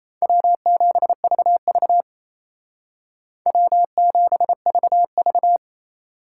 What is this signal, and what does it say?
Morse code, heard as a keyed tone, about 35 wpm, 700 Hz. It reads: W7VV W7VV
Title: Speed Racing - USA 1x2 Call Signs 28wpm